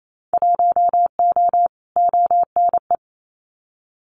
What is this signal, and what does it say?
Morse code, heard as a keyed tone, about 28 wpm, 700 Hz. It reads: 1O ONE
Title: Winter Field Day Exchanges 28wpm